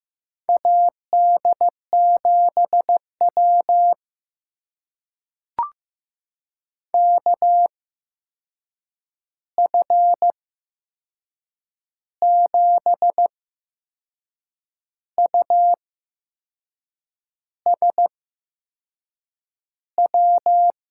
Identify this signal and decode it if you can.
Morse code, heard as a keyed tone, about 15 wpm, spaced as for 6 wpm, 700 Hz. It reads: AD7W KF7USW
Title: USA Call Signs - 8x Character Spacing - 15wpm